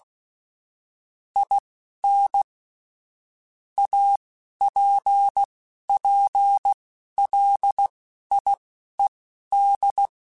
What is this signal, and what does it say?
Morse code, heard as a keyed tone, about 16 wpm, spaced as for 11 wpm, 800 Hz. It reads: IN APPLIED
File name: Technical Subjects / Technical Subjects-mp3s / XC @10